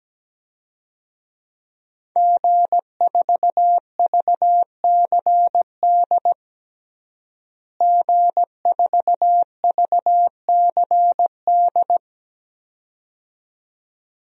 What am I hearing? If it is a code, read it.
Morse code, heard as a keyed tone, about 17 wpm, 700 Hz. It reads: G4VCD G4VCD
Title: DXCC-100 Call Signs - 5 Characters - Repeated 2x - 17wpm